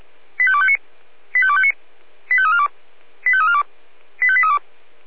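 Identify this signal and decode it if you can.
Five-tone selective call (ZVEI), heard as quick runs of five tones, tones of about 70 ms. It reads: 85259 85259 85342 85342 85832
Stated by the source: ZVEI five-tone call received by SDR.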